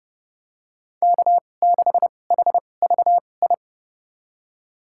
Morse code, heard as a keyed tone, about 30 wpm, spaced as for 22 wpm, 700 Hz. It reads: K6HVI
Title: USA Call Signs - 5 Characters - 2x Character Spacing - 30wpm